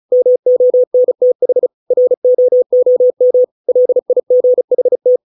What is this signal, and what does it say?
Morse code, heard as a keyed tone, about 35 wpm, 500 Hz. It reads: MONTH ROOM LIGHT